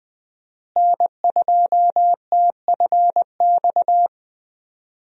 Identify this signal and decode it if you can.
Morse code, heard as a keyed tone, about 20 wpm, 700 Hz. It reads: N2TFX
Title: USA Call Signs - 5 Characters - Repeated 2x - 20wpm